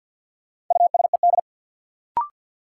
Morse code, heard as a keyed tone, about 50 wpm, 700 Hz. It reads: USED
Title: Sets of 2 Words - New Word Drill - 50wpm